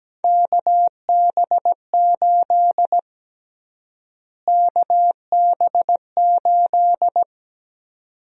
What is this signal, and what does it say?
Morse code, heard as a keyed tone, about 17 wpm, 700 Hz. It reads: KB8 KB8
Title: USA Call Sign Prefixes - Repeated 2x - 17wpm